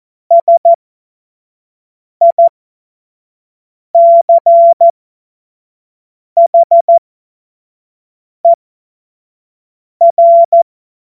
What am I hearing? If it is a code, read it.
Morse code, heard as a keyed tone, about 14 wpm, spaced as for 5 wpm, 680 Hz. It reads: SICHER
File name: Morsekurs / Witze / Witz 04. Oktober 2022 25 F70